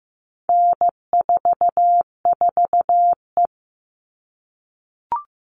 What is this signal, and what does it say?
Morse code, heard as a keyed tone, about 15 wpm, 700 Hz. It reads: N44E
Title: All Cut Numbers - 4-Digit 15wpm